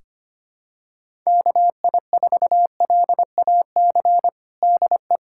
Morse code, heard as a keyed tone, about 25 wpm, 700 Hz. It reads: KI4LAC DE
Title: USA Callsign DE Callsign - 6 Characters - 25wpm